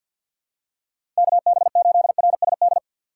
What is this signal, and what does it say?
Morse code, heard as a keyed tone, about 52 wpm, 700 Hz. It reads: KB8RSD
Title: USA Call Signs - 6 Characters - 3x Character Spacing - 50wpm